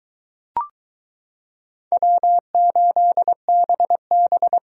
Morse code, heard as a keyed tone, about 23 wpm, 700 Hz. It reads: W8BB